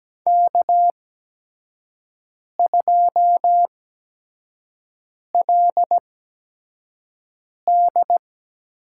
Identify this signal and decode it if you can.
Morse code, heard as a keyed tone, about 17 wpm, spaced as for 5 wpm, 700 Hz. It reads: K2LD
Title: USA Call Signs - 4 Characters - 8x Character Spacing - 17wpm